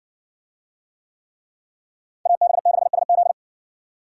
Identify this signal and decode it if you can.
Morse code, heard as a keyed tone, about 57 wpm, 700 Hz. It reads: AB6SZ